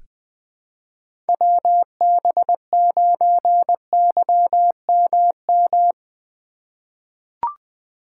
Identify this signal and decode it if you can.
Morse code, heard as a keyed tone, about 21 wpm, 700 Hz. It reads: WB9YMM
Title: USA Call Signs - 6 Characters - 3x Character Spacing - 20wpm